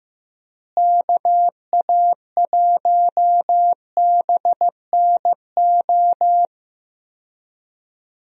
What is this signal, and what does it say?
Morse code, heard as a keyed tone, about 15 wpm, 700 Hz. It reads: KA1BNO
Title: USA Call Signs - 6 Characters - Repeated 3x - 15wpm